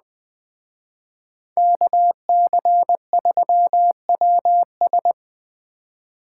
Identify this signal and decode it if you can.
Morse code, heard as a keyed tone, about 20 wpm, 700 Hz. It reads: KC3WS
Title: Speed Racing - USA 2x2 Call Signs 17wpm